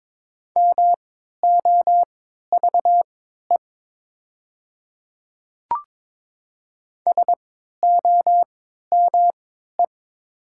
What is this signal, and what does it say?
Morse code, heard as a keyed tone, about 22 wpm, spaced as for 12 wpm, 700 Hz. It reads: MOVE SOME
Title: 3x Character Spacing - Top 500 Words - 22wpm